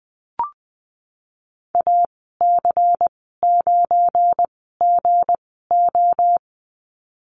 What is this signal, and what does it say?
Morse code, heard as a keyed tone, about 19 wpm, spaced as for 14 wpm, 700 Hz. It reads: AC9GO
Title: 200 Lesson 32g - Letter k - Call Signs 20wpm